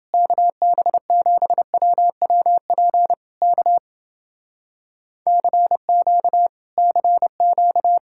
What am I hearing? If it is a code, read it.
Morse code, heard as a keyed tone, about 29 wpm, 700 Hz. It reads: KB7WWP K CQ CQ